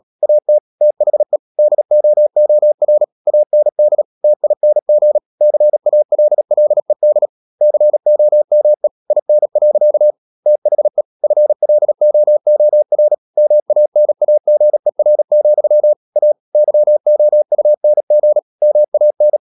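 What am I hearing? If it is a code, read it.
Morse code, heard as a keyed tone, about 37 wpm, 600 Hz. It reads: AT THE DOOR AND TING CALLED COME IN. THE FLOOR MANAGER, A YOUNG MAN
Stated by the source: Morse code practice at 35–39 wpm sending English text of Ken Follett's novel Never, from CW med Gnister.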